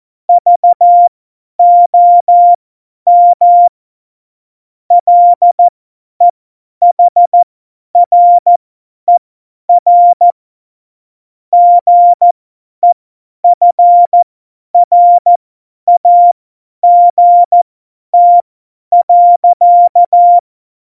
Morse code, heard as a keyed tone, about 14 wpm, spaced as for 10 wpm, 700 Hz. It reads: VOM LEHRER GEFRAGT.
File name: Morsekurs / Witze / Witz 12. November 2019, 50 F70